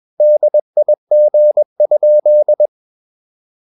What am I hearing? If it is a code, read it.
Morse code, heard as a keyed tone, about 21 wpm, 600 Hz. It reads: DIG?
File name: Ferie QRS 21wpm